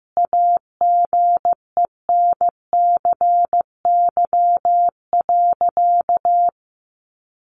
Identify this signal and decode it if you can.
Morse code, heard as a keyed tone, about 15 wpm, 700 Hz. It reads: AGENCY.
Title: Sentences from Top 1300 Words 15wpm